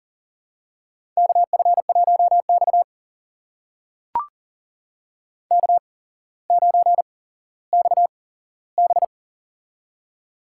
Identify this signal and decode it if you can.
Morse code, heard as a keyed tone, about 38 wpm, spaced as for 17 wpm, 700 Hz. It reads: KF1X K9XB
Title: USA Call Signs - 4 Characters - 8x Character Spacing - 40wpm